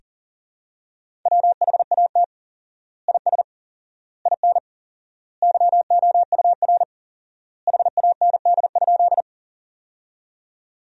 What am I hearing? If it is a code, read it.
Morse code, heard as a keyed tone, about 40 wpm, 700 Hz. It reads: WHAT IS IN YOUR HAND?
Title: 4x Word-Spacing - Sentences from Top 200 Words - 40wpm